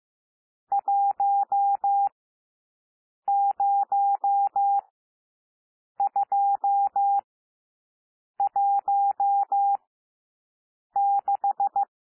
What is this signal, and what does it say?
Morse code, heard as a keyed tone, about 15 wpm, spaced as for 6 wpm, 800 Hz. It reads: 10216
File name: cwmp3 / N1 SOUND NUMBERS 15 wpm